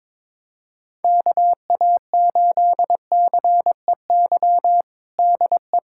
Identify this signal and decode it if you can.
Morse code, heard as a keyed tone, about 22 wpm, 700 Hz. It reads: KA8CEY DE
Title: USA Callsign DE Callsign - 6 Characters - 22wpm